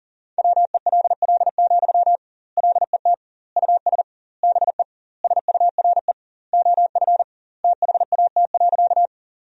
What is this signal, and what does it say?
Morse code, heard as a keyed tone, about 40 wpm, 700 Hz. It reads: WELL, LET US BE SURE OF THAT.
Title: 2x Word-Spacing - Sentences from Top 300 Words - 40wpm